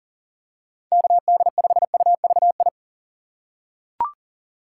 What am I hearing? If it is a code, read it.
Morse code, heard as a keyed tone, about 40 wpm, 700 Hz. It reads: KD5UVI
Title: USA Call Signs - 6 Characters - 4x Character Spacing - 40wpm